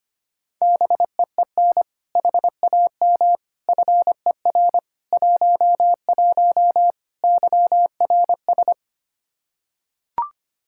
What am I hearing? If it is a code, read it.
Morse code, heard as a keyed tone, about 25 wpm, 700 Hz. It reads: BEEN HAM FER 11 YRS
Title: Practice QSO - Exchange 3 - More Info - 25wpm